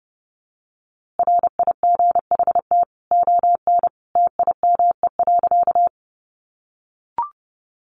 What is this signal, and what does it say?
Morse code, heard as a keyed tone, about 30 wpm, 700 Hz. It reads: RIGHT ON TIME.
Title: A Sound Of Thunder 30wpm